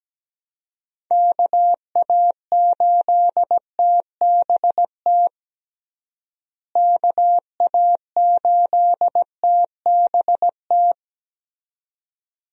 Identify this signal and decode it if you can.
Morse code, heard as a keyed tone, about 17 wpm, 700 Hz. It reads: KA8TBT KA8TBT
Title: USA Call Signs - 6 Characters - Repeated 2x - 17wpm